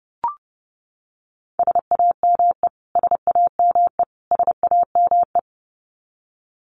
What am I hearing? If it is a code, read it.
Morse code, heard as a keyed tone, about 30 wpm, 700 Hz. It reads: SAME SAME SAME
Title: Top 200 Words Repeated 30wpm